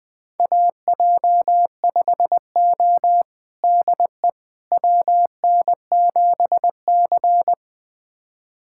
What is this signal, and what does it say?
Morse code, heard as a keyed tone, about 20 wpm, 700 Hz. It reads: AJ5O DE WN7C
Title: USA Callsign DE Callsign - 4 Characters - 20wpm